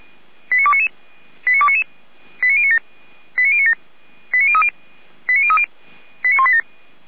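Five-tone selective call (ZVEI), heard as quick runs of five tones, tones of about 70 ms. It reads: 89299 89299 89097 89097 89030 89030 89187
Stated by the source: ZVEI five-tone call 89294 received by SDR.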